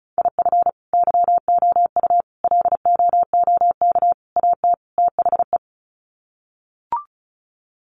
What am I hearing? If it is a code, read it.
Morse code, heard as a keyed tone, about 34 wpm, 700 Hz. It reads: IF YOU LOOK AT THE
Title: Sets of 5 Words 35wpm